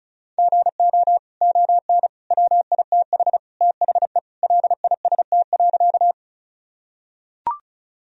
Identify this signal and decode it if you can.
Morse code, heard as a keyed tone, about 36 wpm, 700 Hz. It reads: GO ON WITH THE LIST.
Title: Sentences from Top 1000 Words - Review Part 2 - 35wpm